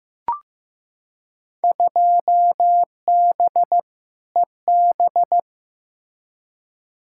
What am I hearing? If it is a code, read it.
Morse code, heard as a keyed tone, about 15 wpm, 700 Hz. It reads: 2B EB
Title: Field Day Exchanges - 15wpm